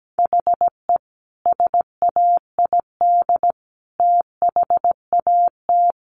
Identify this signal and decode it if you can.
Morse code, heard as a keyed tone, about 17 wpm, 700 Hz. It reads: HE SAID THAT
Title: Sentences from Top 1200 Words 17wpm